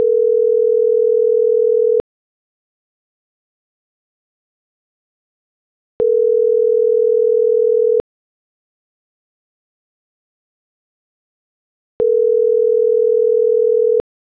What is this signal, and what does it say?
Ringback tone (North America), 440 + 480 Hz.